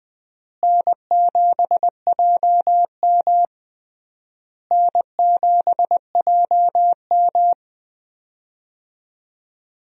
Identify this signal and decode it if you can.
Morse code, heard as a keyed tone, about 20 wpm, 700 Hz. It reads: N7JM N7JM
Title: USA Call Signs - 4 Characters - Repeated 3x - 20wpm